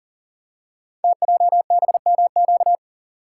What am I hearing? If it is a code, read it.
Morse code, heard as a keyed tone, about 41 wpm, 700 Hz. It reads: TJBMQ
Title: Code Groups - 5 Letters - 40wpm